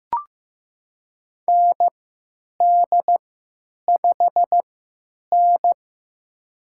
Morse code, heard as a keyed tone, about 15 wpm, spaced as for 9 wpm, 700 Hz. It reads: ND5N
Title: USA Call Signs - 4 Characters - 3x Character Spacing - 15wpm